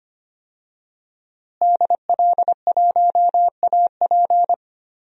Morse code, heard as a keyed tone, about 25 wpm, 700 Hz. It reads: DL1AP
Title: DXCC-100 Call Signs - 5 Characters - Repeated 2x - 25wpm